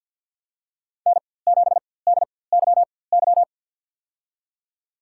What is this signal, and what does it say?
Morse code, heard as a keyed tone, about 50 wpm, spaced as for 23 wpm, 700 Hz. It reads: N7DYY
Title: USA Call Signs - 5 Characters - 4x Character Spacing - 50wpm